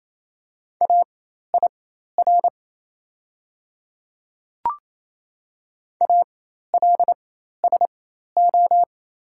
Morse code, heard as a keyed tone, about 28 wpm, spaced as for 13 wpm, 700 Hz. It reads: AIR ALSO
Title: 4x Character Spacing - Top 200 Words - 28wpm